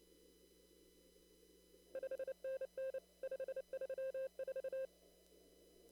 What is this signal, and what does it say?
Morse code, heard as a keyed tone, about 28 wpm, 530 Hz. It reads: 5NN 534
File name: CW / kt5j wpx cw-op-k5pi